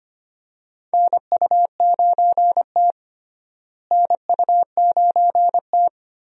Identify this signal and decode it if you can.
Morse code, heard as a keyed tone, about 25 wpm, 700 Hz. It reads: NU9T NU9T
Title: USA Call Signs - 4 Characters - Repeated 2x - 25wpm